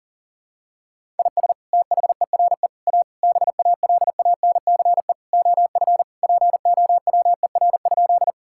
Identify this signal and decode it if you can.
Morse code, heard as a keyed tone, about 40 wpm, 700 Hz. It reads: IS THERE A BALANCE OF POWER?
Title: Sentences from Top 1000 Words - Review Part 6 - 40wpm